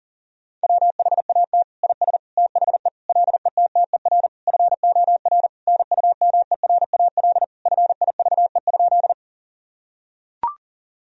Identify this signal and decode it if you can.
Morse code, heard as a keyed tone, about 40 wpm, 700 Hz. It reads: WHAT IS THE LETTER FOR NUMERAL FIVE?
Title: Sentences from Top 1000 Words - Review Part 2 - 40wpm